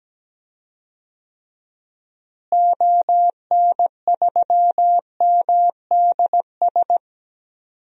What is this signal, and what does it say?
Morse code, heard as a keyed tone, about 17 wpm, 700 Hz. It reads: ON3MDS
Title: DXCC-100 Call Signs - 6 Characters - Repeated 2x - 17wpm